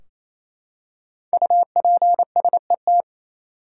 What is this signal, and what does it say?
Morse code, heard as a keyed tone, about 28 wpm, 700 Hz. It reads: UPSET
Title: Five Letter Words - 28wpm